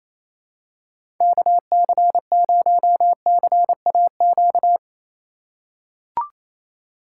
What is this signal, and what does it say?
Morse code, heard as a keyed tone, about 28 wpm, 700 Hz. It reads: KC0CAQ
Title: USA Call Signs - 6 Characters - 4x Character Spacing - 28wpm